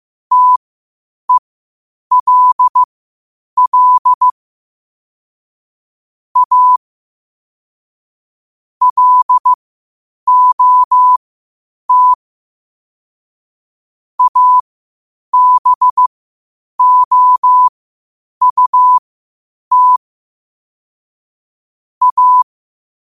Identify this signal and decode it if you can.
Morse code, heard as a keyed tone, about 15 wpm, spaced as for 8 wpm, 1000 Hz. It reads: TELL A LOT ABOUT A
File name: qotdfiles / 8wpm